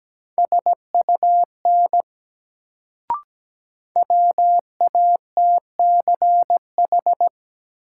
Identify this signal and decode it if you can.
Morse code, heard as a keyed tone, about 17 wpm, 700 Hz. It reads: SUN WATCH
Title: Sentences from Top 300 Words - New Word Drill - 17wpm